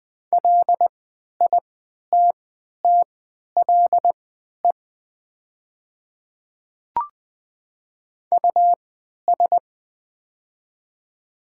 Morse code, heard as a keyed tone, about 20 wpm, spaced as for 11 wpm, 700 Hz. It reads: LITTLE US
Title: 3x Character Spacing - Top 200 Words - 20wpm